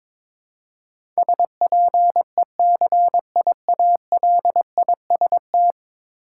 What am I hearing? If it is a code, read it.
Morse code, heard as a keyed tone, about 22 wpm, 700 Hz. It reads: SPECIALIST
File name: practice / 22wpm / 10 Letter Words - First 3 Letters Given - 22wpm